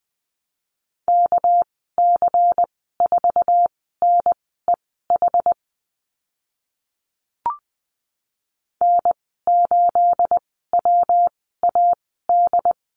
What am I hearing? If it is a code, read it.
Morse code, heard as a keyed tone, about 20 wpm, spaced as for 14 wpm, 700 Hz. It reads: KC4NEH N8WAD
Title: 238 Lesson 38g - Letter q - Call Signs 20wpm